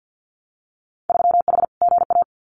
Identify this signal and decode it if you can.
Morse code, heard as a keyed tone, about 50 wpm, 700 Hz. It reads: 3H GA
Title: Winter Field Day Exchanges Encore 50wpm